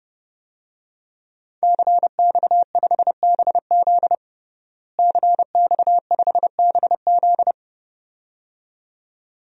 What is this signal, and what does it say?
Morse code, heard as a keyed tone, about 32 wpm, 700 Hz. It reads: CX5BZ CX5BZ
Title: DXCC-100 Call Signs - 5 Characters - Repeated 2x - 30wpm